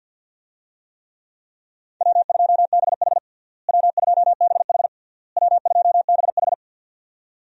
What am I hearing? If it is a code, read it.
Morse code, heard as a keyed tone, about 49 wpm, 700 Hz. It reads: W2BH W2BH W2BH
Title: USA Call Signs - 4 Characters - Repeated 3x - 50wpm